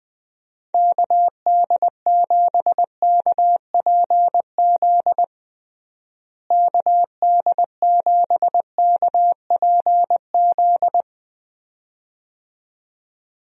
Morse code, heard as a keyed tone, about 20 wpm, 700 Hz. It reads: KD7KPZ KD7KPZ
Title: USA Call Signs - 6 Characters - Repeated 2x - 20wpm